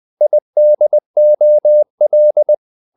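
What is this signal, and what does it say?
Morse code, heard as a keyed tone, about 20 wpm, 600 Hz. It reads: IDOL
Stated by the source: Morse code practice of short Danish words at 20 wpm, from CW med Gnister.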